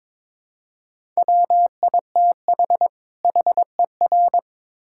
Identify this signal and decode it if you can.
Morse code, heard as a keyed tone, about 22 wpm, 700 Hz. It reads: WITH HER
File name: Build A Phrase - 2-Word Phrases - 22wpm